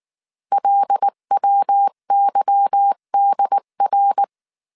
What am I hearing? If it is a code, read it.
Morse code, heard as a keyed tone, about 19 wpm, 800 Hz. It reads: LWYDR